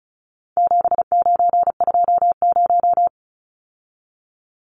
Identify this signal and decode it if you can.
Morse code, heard as a keyed tone, about 34 wpm, 700 Hz. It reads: 7920
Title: Numbers - 4-Digit 35wpm